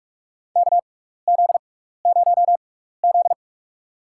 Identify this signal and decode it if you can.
Morse code, heard as a keyed tone, about 47 wpm, spaced as for 16 wpm, 700 Hz. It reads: KZ0Z
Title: USA Call Signs - 4 Characters - 6x Character Spacing - 45wpm